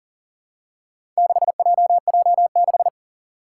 Morse code, heard as a keyed tone, about 40 wpm, 700 Hz. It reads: 6JJ6
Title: Four Challenging Letter-Number Combinations 40wpm